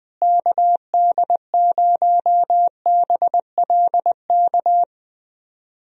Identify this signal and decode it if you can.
Morse code, heard as a keyed tone, about 21 wpm, 700 Hz. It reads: KD0BLK
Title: USA Call Signs - 6 Characters - Repeated 3x - 20wpm